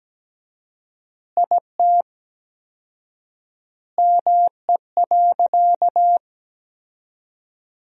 Morse code, heard as a keyed tone, about 17 wpm, 700 Hz. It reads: IT ME.
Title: 4x Word-Spacing - Sentences from Top 200 Words - 17wpm